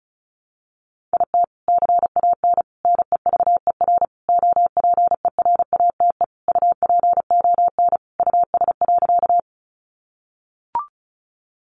Morse code, heard as a keyed tone, about 34 wpm, 700 Hz. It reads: IT CAN NEVER OPERATE UPON US.
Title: Sentences from Top 1500 Words 35wpm